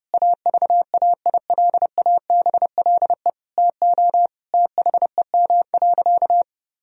Morse code, heard as a keyed tone, about 30 wpm, 700 Hz. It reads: AVAILABLE TO THEM.
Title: Sentences from Top 1000 Words - Review Part 3 - 30wpm